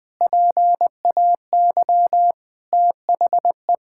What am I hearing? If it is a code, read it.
Morse code, heard as a keyed tone, about 20 wpm, 700 Hz. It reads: PAY THE
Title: Sentences from Top 1000 Words - Review Part 6 - 20wpm